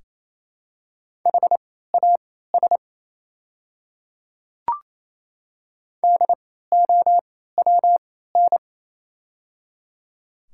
Morse code, heard as a keyed tone, about 28 wpm, spaced as for 16 wpm, 700 Hz. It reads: HAS DOWN
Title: 3x Character Spacing - Top 100 Words - 28wpm